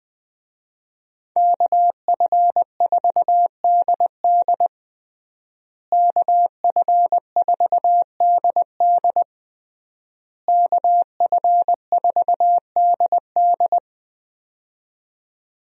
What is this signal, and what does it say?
Morse code, heard as a keyed tone, about 20 wpm, 700 Hz. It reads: KF4DD KF4DD KF4DD
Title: USA Call Signs - 5 Characters - Repeated 3x - 20wpm